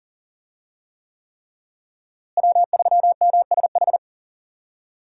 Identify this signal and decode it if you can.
Morse code, heard as a keyed tone, about 40 wpm, 700 Hz. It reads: W3MSH
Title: USA Call Signs - 5 Characters - Repeated 3x - 40wpm